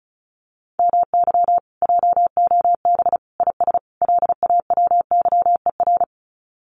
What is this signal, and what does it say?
Morse code, heard as a keyed tone, about 34 wpm, 700 Hz. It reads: MY JOB IS LAWYER
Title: Common QSO Phrases Encore 35wpm